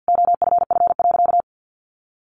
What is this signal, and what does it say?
Morse code, heard as a keyed tone, about 50 wpm, 700 Hz. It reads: OFF.
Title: Sentences from Top 1400 Words 50wpm